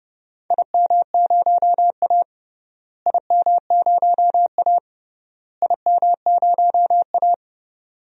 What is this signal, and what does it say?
Morse code, heard as a keyed tone, about 30 wpm, 700 Hz. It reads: IM0A IM0A IM0A
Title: DXCC-100 Call Signs - 4 Characters - Repeated 3x - 30wpm